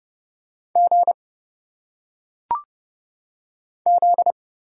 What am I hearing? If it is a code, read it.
Morse code, heard as a keyed tone, about 30 wpm, 700 Hz. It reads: G Z
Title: Single Letters - Phonetic Pronunciation 30wpm